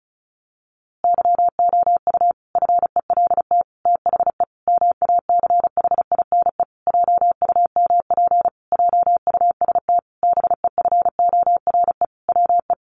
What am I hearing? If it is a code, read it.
Morse code, heard as a keyed tone, about 34 wpm, 700 Hz. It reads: YOU FELT THE MACHINE JUMP JUST BEFORE WE
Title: A Sound Of Thunder 35wpm